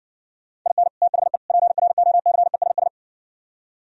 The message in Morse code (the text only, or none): IN THE PROCESS